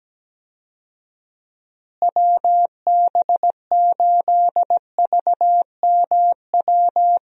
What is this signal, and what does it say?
Morse code, heard as a keyed tone, about 17 wpm, 700 Hz. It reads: WB8VMW
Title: USA Call Signs - 6 Characters - Repeated 2x - 17wpm